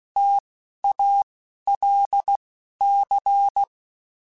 Morse code, heard as a keyed tone, about 16 wpm, spaced as for 11 wpm, 800 Hz. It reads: TALC